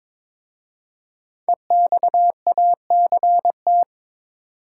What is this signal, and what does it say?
Morse code, heard as a keyed tone, about 22 wpm, 700 Hz. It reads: EXACT
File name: 5 Letter Words - First 3 Letters Given - 22wpm